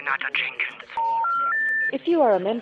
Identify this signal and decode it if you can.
Special information tone, with speech.